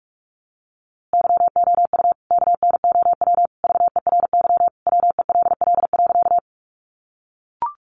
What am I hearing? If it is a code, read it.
Morse code, heard as a keyed tone, about 44 wpm, 700 Hz. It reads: YOU KNOW VERY WELL.
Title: A Sound Of Thunder 45wpm